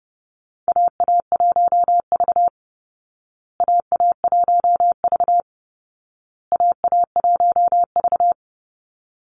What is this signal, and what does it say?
Morse code, heard as a keyed tone, about 30 wpm, 700 Hz. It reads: AA1V AA1V AA1V
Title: Call Signs Repeated 2x - 30wpm